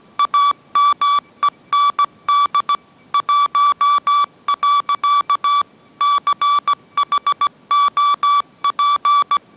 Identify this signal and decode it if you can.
Morse code, heard as a keyed tone, about 17 wpm, 1150 Hz. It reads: AMEND 1. CHOP